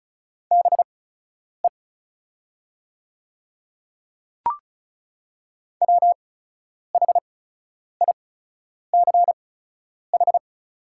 Morse code, heard as a keyed tone, about 34 wpm, spaced as for 10 wpm, 690 Hz. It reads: BE WHICH
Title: 8x Character Spacing - Top 100 Words - 35wpm